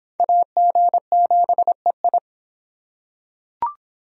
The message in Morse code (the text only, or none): AG7EI